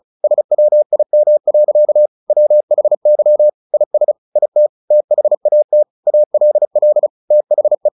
Morse code, heard as a keyed tone, about 34 wpm, 600 Hz. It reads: SWIM. WHY IS IT THAT ALL THE
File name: the old man and the sea 2 35wpm